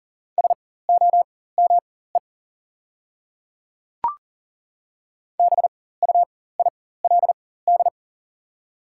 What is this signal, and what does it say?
Morse code, heard as a keyed tone, about 39 wpm, spaced as for 19 wpm, 700 Hz. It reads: SOME BUILD
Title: 4x Character Spacing - Top 300 Words - 40wpm